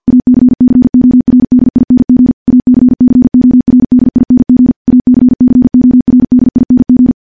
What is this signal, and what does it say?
Morse code, heard as a keyed tone, about 50 wpm, 250 Hz. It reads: ACCORDING ACCORDING ACCORDING
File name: morse-training-50WPM